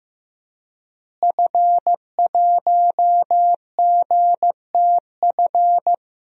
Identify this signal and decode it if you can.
Morse code, heard as a keyed tone, about 15 wpm, 700 Hz. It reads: F1GTF
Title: DXCC-100 Call Signs - 5 Characters - Repeated 2x - 15wpm